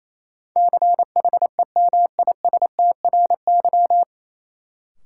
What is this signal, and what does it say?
Morse code, heard as a keyed tone, about 28 wpm, 700 Hz. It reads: CHEMISTRY
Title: Sentences from Top 1000 Words - Review Part 6 - New Word Drill - 28wpm